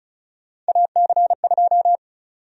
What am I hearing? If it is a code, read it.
Morse code, heard as a keyed tone, about 36 wpm, 700 Hz. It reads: AC2
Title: USA Call Sign Prefixes - 35wpm